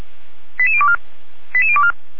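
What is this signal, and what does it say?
Five-tone selective call (ZVEI), heard as quick runs of five tones, tones of about 70 ms. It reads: 88024 88024
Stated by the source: ZVEI five-tone selective call received on SDR.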